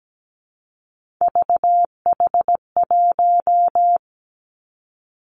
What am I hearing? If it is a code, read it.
Morse code, heard as a keyed tone, about 17 wpm, 700 Hz. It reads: VH1
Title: Three Challenging Letter-Number Combinations 17wpm